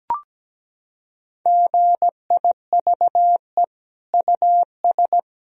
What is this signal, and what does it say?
Morse code, heard as a keyed tone, about 17 wpm, 700 Hz. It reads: GIVE US
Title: Sentences from Top 1000 Words - Review Part 4 - 17wpm